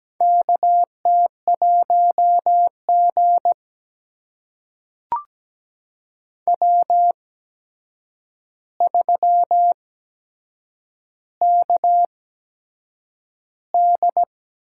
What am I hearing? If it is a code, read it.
Morse code, heard as a keyed tone, about 17 wpm, spaced as for 7 wpm, 700 Hz. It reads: KT1G W3KD
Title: USA Call Signs - 4 Characters - 8x Character Spacing - 17wpm